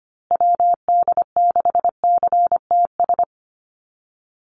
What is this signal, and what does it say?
Morse code, heard as a keyed tone, about 25 wpm, 700 Hz. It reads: WD6CTS
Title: USA Call Signs Repeated 3x 25wpm